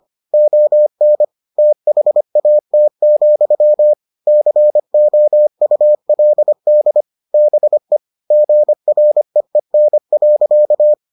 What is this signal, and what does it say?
Morse code, heard as a keyed tone, about 25 wpm, 600 Hz. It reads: ON THAT, COULD BE GREEN.